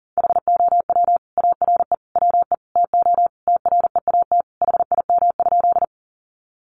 Morse code, heard as a keyed tone, about 40 wpm, 700 Hz. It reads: HOW ARE WE TO TREAT HIM?
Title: Sentences from Top 1300 Words 40wpm